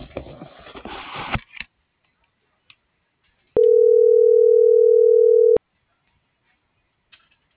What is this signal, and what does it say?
Ringback tone (North America), 440 + 480 Hz.